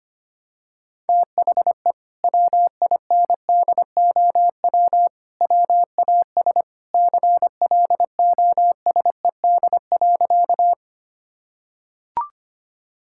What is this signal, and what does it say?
Morse code, heard as a keyed tone, about 25 wpm, 700 Hz. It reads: THE WINDOW WAS CLOSED.
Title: Sentences from Top 1600 Words 25wpm